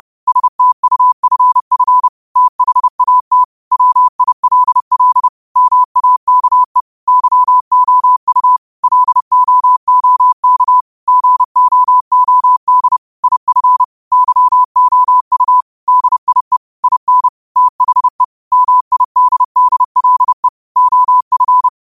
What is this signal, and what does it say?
Morse code, heard as a keyed tone, about 30 wpm, 1000 Hz. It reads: STUFF THAT WILL MAKE YOU LOOK GOOD IF YOU DIE IN THE MIDDLE OF
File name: qotdfiles / 30wpm